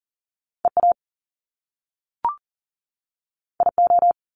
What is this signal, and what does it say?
Morse code, heard as a keyed tone, about 39 wpm, 700 Hz. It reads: EA IO
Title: Top Two Letter Combinations in Words 40wpm